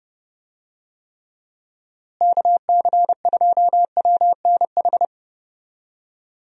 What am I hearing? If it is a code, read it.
Morse code, heard as a keyed tone, about 30 wpm, 700 Hz. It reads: KC2WNH
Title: USA Call Signs - 6 Characters - Repeated 3x - 30wpm